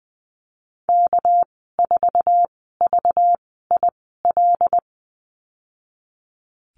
Morse code, heard as a keyed tone, about 20 wpm, spaced as for 14 wpm, 700 Hz. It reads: K4VIL
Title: 245 Lesson 39g - Letter z - Call Signs 20wpm